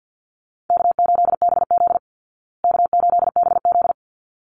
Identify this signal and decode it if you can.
Morse code, heard as a keyed tone, about 50 wpm, 700 Hz. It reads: K8BZ K8BZ
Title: Easy Call Signs Repeated 2x - 50wpm